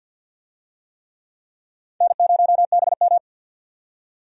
Morse code, heard as a keyed tone, about 51 wpm, 700 Hz. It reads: N0BM
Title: USA Call Signs - 4 Characters - 50wpm